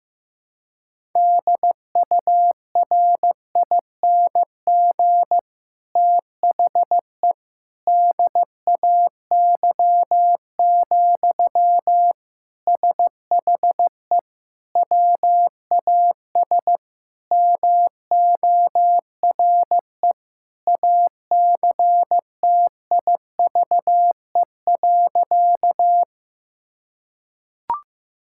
Morse code, heard as a keyed tone, about 15 wpm, 700 Hz. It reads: DURING THE DAY, SHE WAS MORE ACTIVE.
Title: Sentences from Top 1000 Words - Review Part 6 - 15wpm